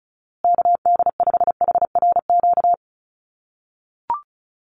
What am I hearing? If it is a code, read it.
Morse code, heard as a keyed tone, about 34 wpm, 700 Hz. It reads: KD5HRQ
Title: USA Call Signs Repeated 3x 35wpm